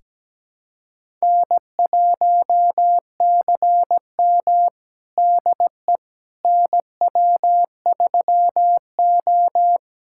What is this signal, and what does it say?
Morse code, heard as a keyed tone, about 17 wpm, 700 Hz. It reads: N1CM DE NW3O
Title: USA Callsign DE Callsign - 4 Characters - 17wpm